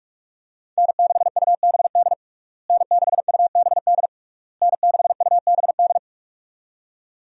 Morse code, heard as a keyed tone, about 42 wpm, 700 Hz. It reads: N6UBD N6UBD N6UBD
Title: USA Call Signs - 5 Characters - Repeated 3x - 45wpm